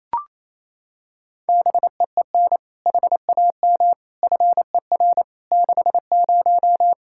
Morse code, heard as a keyed tone, about 28 wpm, 700 Hz. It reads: BEEN HAM FER 60
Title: Practice QSO - Exchange 3 - More Info - 28wpm